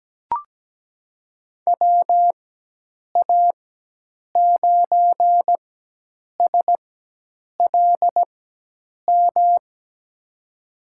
Morse code, heard as a keyed tone, about 17 wpm, spaced as for 8 wpm, 700 Hz. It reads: WA9SLM
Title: USA Call Signs - 6 Characters - 4x Character Spacing - 17wpm